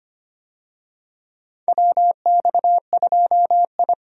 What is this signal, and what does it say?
Morse code, heard as a keyed tone, about 25 wpm, 700 Hz. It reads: WX2I
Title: USA Call Signs - 4 Characters - 25wpm